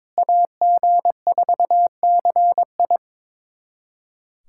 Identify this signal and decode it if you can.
Morse code, heard as a keyed tone, about 21 wpm, 700 Hz. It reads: AG4CI
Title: USA Callsign DE Callsign - 5 Characters - 22wpm